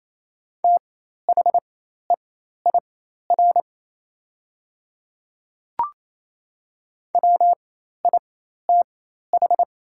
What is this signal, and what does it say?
Morse code, heard as a keyed tone, about 28 wpm, spaced as for 13 wpm, 700 Hz. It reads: THEIR WITH
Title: 4x Character Spacing - Top 100 Words - 28wpm